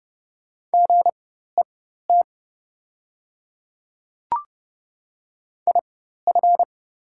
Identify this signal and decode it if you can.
Morse code, heard as a keyed tone, about 30 wpm, spaced as for 14 wpm, 690 Hz. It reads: GET IF
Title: 4x Character Spacing - Top 500 Words - 30wpm